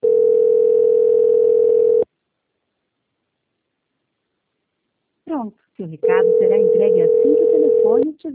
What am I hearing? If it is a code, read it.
Ringback tone (North America), 440 + 480 Hz, with speech.
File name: out- -2002- - - .33